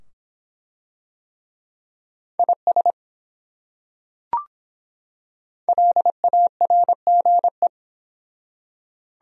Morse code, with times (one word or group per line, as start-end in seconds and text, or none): 2.4-2.9 IS
5.7-7.7 LARGE